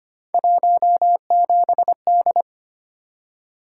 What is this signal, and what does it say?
Morse code, heard as a keyed tone, about 25 wpm, 700 Hz. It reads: 17D
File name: Three Letter-Number Combinations - 25wpm